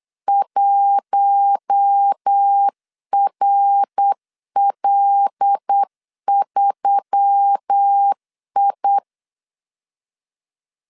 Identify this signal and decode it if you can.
Morse code, heard as a keyed tone, about 8 wpm, 800 Hz. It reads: 1RL3I
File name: 8wpm